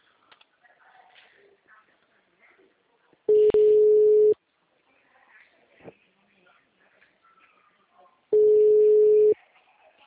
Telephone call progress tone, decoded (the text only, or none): ringback tone